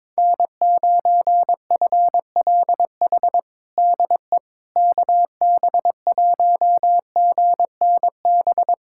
Morse code, heard as a keyed tone, about 22 wpm, 700 Hz. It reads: N9FLH DE KB1GNB
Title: USA Callsign DE Callsign 22wpm